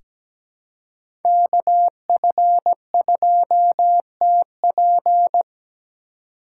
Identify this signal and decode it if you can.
Morse code, heard as a keyed tone, about 17 wpm, 700 Hz. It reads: KF2TP
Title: K1USN SST QSO Practice - 17wpm